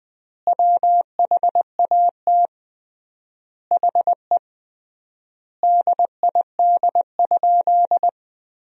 Morse code, heard as a keyed tone, about 21 wpm, 700 Hz. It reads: WHAT HE DID?
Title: 3x Word-Spacing - Sentences from Top 100 Words - 20wpm